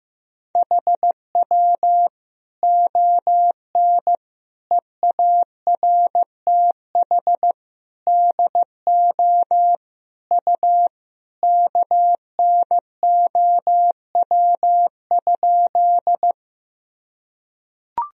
Morse code, heard as a keyed tone, about 16 wpm, 700 Hz. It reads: HW ON EARTH DO U KNOW?
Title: Sentences from Top 200 Words and Abbreviations 15wpm